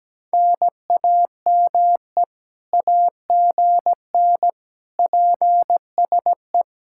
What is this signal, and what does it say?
Morse code, heard as a keyed tone, about 17 wpm, 700 Hz. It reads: NAME AGN PSE
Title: Speed-Racing Common QSO Phrases - 17wpm